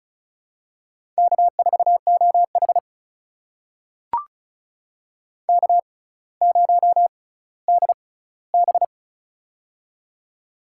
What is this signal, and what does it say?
Morse code, heard as a keyed tone, about 36 wpm, spaced as for 18 wpm, 700 Hz. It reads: K4OH K0DB
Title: USA Call Signs - 4 Characters - 6x Character Spacing - 35wpm